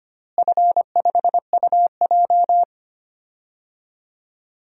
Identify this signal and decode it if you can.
Morse code, heard as a keyed tone, about 25 wpm, 700 Hz. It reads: F5UJ
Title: DXCC-100 Call Signs - 4 Characters - Repeated 2x - 25wpm